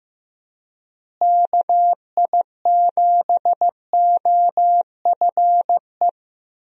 Morse code, heard as a keyed tone, about 15 wpm, 700 Hz. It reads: KI7OFE
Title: USA Call Signs - 6 Characters - Repeated 3x - 15wpm